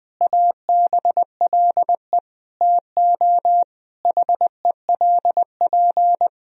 Morse code, heard as a keyed tone, about 20 wpm, 700 Hz. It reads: ABLE TO HELP
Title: Sentences from Top 1000 Words - Review Part 2 - 20wpm